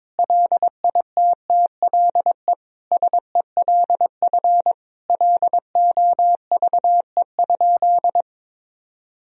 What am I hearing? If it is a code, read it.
Morse code, heard as a keyed tone, about 22 wpm, 700 Hz. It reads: LITTLE SELF LOVE?
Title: Sentences from Top 1000 Words - Review Part 1 - 22wpm